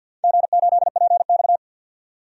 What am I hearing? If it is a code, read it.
Morse code, heard as a keyed tone, about 49 wpm, 700 Hz. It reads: G8PX
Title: DXCC-100 Call Signs - 4 Characters - 50wpm